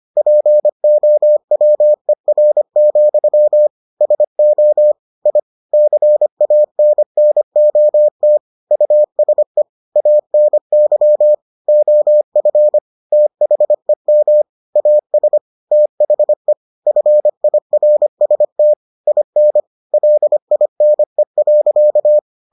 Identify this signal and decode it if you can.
Morse code, heard as a keyed tone, about 25 wpm, 600 Hz. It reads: POWER, SO I CANNOT USE ANY OF THEM AS THE FIRST IN LINE.